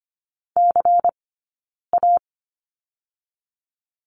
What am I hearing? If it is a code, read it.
Morse code, heard as a keyed tone, about 25 wpm, 700 Hz. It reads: C A
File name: course / 25:10wpm / 092 Lesson 16d - Letter c - Two Characters 25wpm